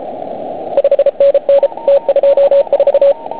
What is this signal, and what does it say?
Morse code, heard as a keyed tone, about 34 wpm, 560 Hz, with radio noise. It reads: 5NN T24